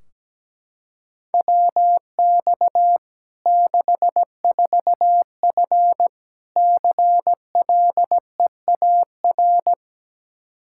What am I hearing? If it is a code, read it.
Morse code, heard as a keyed tone, about 17 wpm, 700 Hz. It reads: WX 64F CLEAR
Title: Practice QSO - Exchange 2 - Info - 17wpm